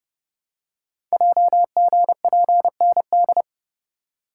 Morse code, heard as a keyed tone, about 30 wpm, 700 Hz. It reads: JGPND
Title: Code Groups - 5 Letters - 30wpm